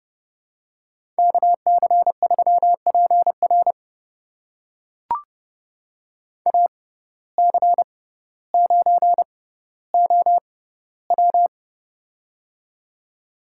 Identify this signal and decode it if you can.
Morse code, heard as a keyed tone, about 32 wpm, spaced as for 16 wpm, 700 Hz. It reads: KC3PR AC9OW
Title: USA Call Signs - 5 Characters - 6x Character Spacing - 30wpm